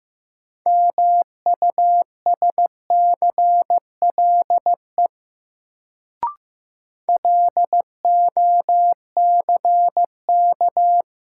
Morse code, heard as a keyed tone, about 15 wpm, 700 Hz. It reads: MUSCLE LOCK
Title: Sentences from Top 1300 Words - New Word Drill Alt - 15wpm